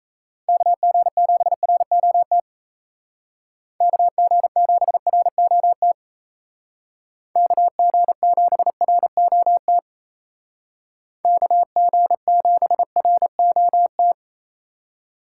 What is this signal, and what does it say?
Morse code, heard as a keyed tone, about 37 wpm, 700 Hz. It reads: KG7ROT KG7ROT KG7ROT KG7ROT